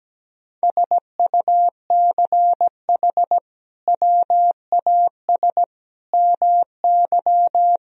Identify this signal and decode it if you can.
Morse code, heard as a keyed tone, about 17 wpm, 700 Hz. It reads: SUCH WAS MY
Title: Sentences from Top 1000 Words - Review Part 1 - 17wpm